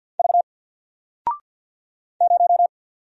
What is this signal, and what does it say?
Morse code, heard as a keyed tone, about 48 wpm, 700 Hz. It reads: V 0
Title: Single Letter-Number 50wpm